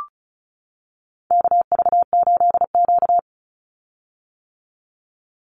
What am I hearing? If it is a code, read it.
Morse code, heard as a keyed tone, about 34 wpm, 700 Hz. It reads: KV8Q
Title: Call Signs - 35wpm